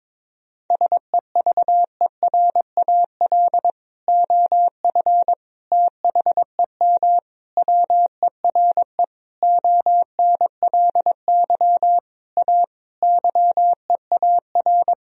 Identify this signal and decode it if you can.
Morse code, heard as a keyed tone, about 21 wpm, 700 Hz. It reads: SEVERAL OF THEM WERE ONLY A YEAR
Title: Sentences from Top 1000 Words - Review Part 2 - 22wpm